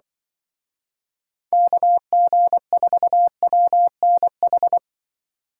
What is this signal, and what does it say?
Morse code, heard as a keyed tone, about 24 wpm, 700 Hz. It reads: KG4WNH
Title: Speed Racing - USA 2x3 Call Signs 20wpm